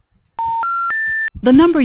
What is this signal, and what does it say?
Special information tone, with speech.